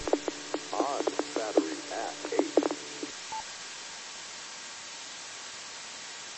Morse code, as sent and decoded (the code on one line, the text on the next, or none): none